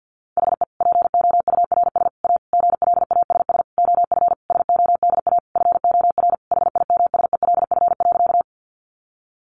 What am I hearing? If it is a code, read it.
Morse code, heard as a keyed tone, about 50 wpm, 700 Hz. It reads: HE POURS A GLASS OF SODA FOR HIMSELF.
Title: Additional Easy Phrases 50wpm